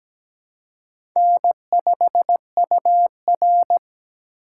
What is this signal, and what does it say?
Morse code, heard as a keyed tone, about 17 wpm, 700 Hz. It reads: N5UR
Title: USA Call Signs - 4 Characters - 2x Character Spacing - 17wpm